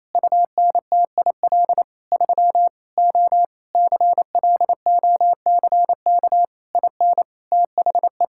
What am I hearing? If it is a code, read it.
Morse code, heard as a keyed tone, about 28 wpm, 700 Hz. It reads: UNTIL 3 O CLOCK IN THE
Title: Pangram Challenge v2 - 28wpm